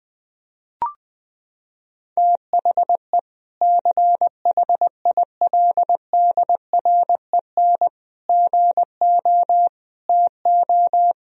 Morse code, heard as a keyed tone, about 21 wpm, 700 Hz. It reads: THE CHILDREN GO TO
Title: Sentences from Top 1000 Words - Review Part 1 - 20wpm